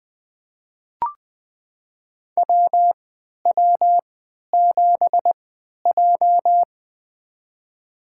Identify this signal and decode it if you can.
Morse code, heard as a keyed tone, about 20 wpm, spaced as for 11 wpm, 700 Hz. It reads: WW7J
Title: USA Call Signs - 4 Characters - 3x Character Spacing - 20wpm